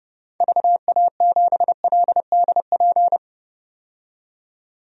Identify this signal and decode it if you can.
Morse code, heard as a keyed tone, about 30 wpm, 700 Hz. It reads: VA7LDP
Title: DXCC-100 Call Signs - 6 Characters - 30wpm